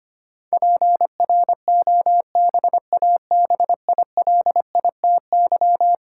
Morse code, heard as a keyed tone, about 25 wpm, 700 Hz. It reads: PROBABILITY
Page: morse-code-ninja-practice-25wpm